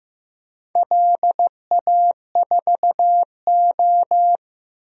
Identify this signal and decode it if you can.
Morse code, heard as a keyed tone, about 15 wpm, 700 Hz. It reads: LA4O
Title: DXCC-100 Call Signs - 4 Characters - 15wpm